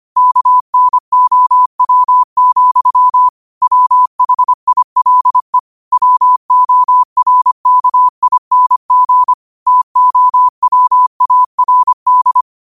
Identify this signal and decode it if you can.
Morse code, heard as a keyed tone, about 25 wpm, 1000 Hz. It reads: KNOW, WHILE WORKING TOWARD